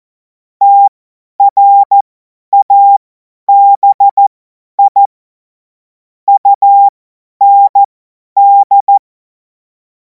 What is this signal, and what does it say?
Morse code, heard as a keyed tone, about 14 wpm, spaced as for 10 wpm, 800 Hz. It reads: TRABI UND
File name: Morsekurs / Witze / Witz 07. Juli 2026 50 F70